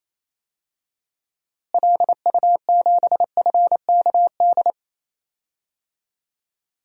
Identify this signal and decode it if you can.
Morse code, heard as a keyed tone, about 28 wpm, 700 Hz. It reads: LU7FKD
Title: DXCC-100 Call Signs - 6 Characters - Repeated 2x - 28wpm